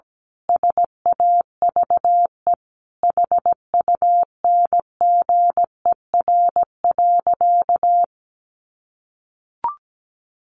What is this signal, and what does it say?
Morse code, heard as a keyed tone, about 17 wpm, 700 Hz. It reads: SAVE HUNGER.
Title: A Sound Of Thunder 17wpm